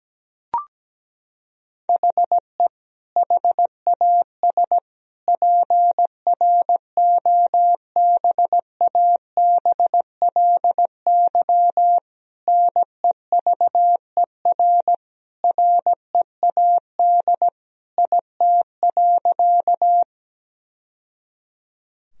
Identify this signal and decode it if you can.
Morse code, heard as a keyed tone, about 17 wpm, 700 Hz. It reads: HE HAS PROBABLY NEVER READ IT.
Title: Sentences from Top 1000 Words - Review Part 4 - 17wpm